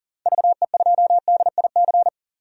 Morse code, heard as a keyed tone, about 37 wpm, 700 Hz. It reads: VE2DIC
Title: DXCC-100 Call Signs - 6 Characters - 40wpm